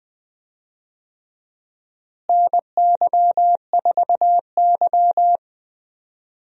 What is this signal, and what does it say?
Morse code, heard as a keyed tone, about 20 wpm, 700 Hz. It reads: NY4Y
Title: USA Call Signs - 4 Characters - Repeated 2x - 20wpm